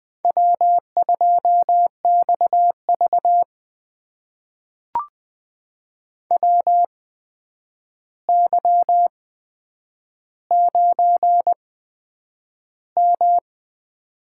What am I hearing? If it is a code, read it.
Morse code, heard as a keyed tone, about 20 wpm, spaced as for 9 wpm, 700 Hz. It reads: W2XV WY9M
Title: USA Call Signs - 4 Characters - 8x Character Spacing - 20wpm